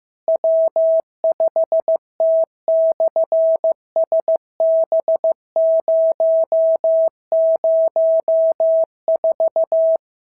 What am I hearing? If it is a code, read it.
Morse code, heard as a keyed tone, about 15 wpm, 650 Hz. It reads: W5T/SB004
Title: SOTA QSO Practice 15wpm